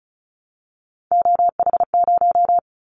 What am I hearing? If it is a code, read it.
Morse code, heard as a keyed tone, about 36 wpm, 700 Hz. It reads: OH0
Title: Callsign Prefixes for Top 100 DXCC - 35wpm